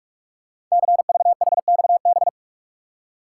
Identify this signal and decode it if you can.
Morse code, heard as a keyed tone, about 44 wpm, 700 Hz. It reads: CVHXB
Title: Code Groups - 5 Letters - 45wpm